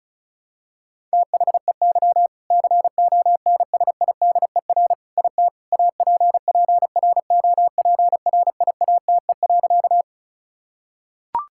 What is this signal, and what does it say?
Morse code, heard as a keyed tone, about 36 wpm, 700 Hz. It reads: THEY CONSIDER IT APPROPRIATE.
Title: Sentences from Top 1000 Words - Review Part 5 - 35wpm